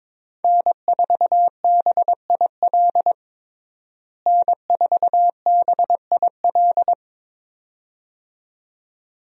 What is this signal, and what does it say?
Morse code, heard as a keyed tone, about 22 wpm, 700 Hz. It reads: N4BIL N4BIL
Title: USA Call Signs - 5 Characters - Repeated 2x - 22wpm